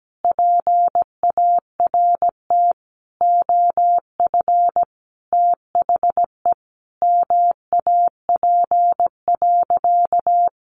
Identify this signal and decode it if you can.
Morse code, heard as a keyed tone, about 17 wpm, 700 Hz. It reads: PART OF THE MAP.
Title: Sentences from Top 400 Words 17wpm